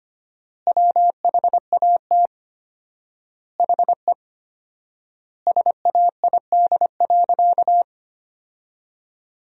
Morse code, heard as a keyed tone, about 25 wpm, 700 Hz. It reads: WHAT HE SAID.
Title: 4x Word-Spacing - Sentences from Top 400 Words - 25wpm